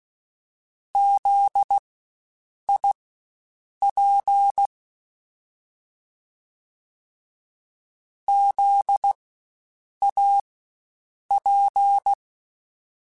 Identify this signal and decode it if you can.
Morse code, heard as a keyed tone, about 16 wpm, spaced as for 7 wpm, 800 Hz. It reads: ZIP ZAP
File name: Spaced Alphas / Spaced Alphas mp3s / Zs sp @15